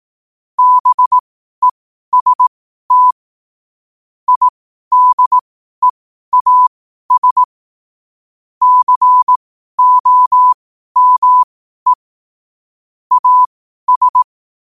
Morse code, heard as a keyed tone, about 18 wpm, spaced as for 12 wpm, 1000 Hz. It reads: BEST IDEAS COME AS
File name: qotdfiles / 12wpm